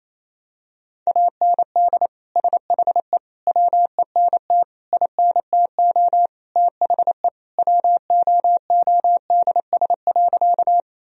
Morse code, heard as a keyed tone, about 28 wpm, 700 Hz. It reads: AND SHE WENT INTO THE WOODS.
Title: Sentences from Top 1000 Words - Review Part 2 - 28wpm